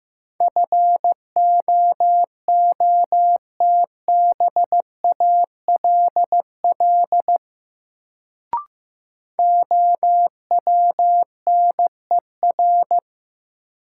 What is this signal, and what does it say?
Morse code, heard as a keyed tone, about 15 wpm, 700 Hz. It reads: FOOTBALL OWNER
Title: Sentences from Top 1600 Words - New Word Drill - 15wpm